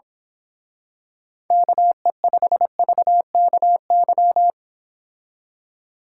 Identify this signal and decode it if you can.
Morse code, heard as a keyed tone, about 25 wpm, 700 Hz. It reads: KE5VKY
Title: Speed Racing - USA 2x3 Call Signs 22wpm